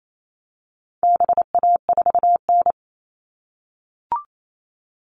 Morse code, heard as a keyed tone, about 28 wpm, 700 Hz. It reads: BA4N
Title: All Cut Numbers - 4-Digit 28wpm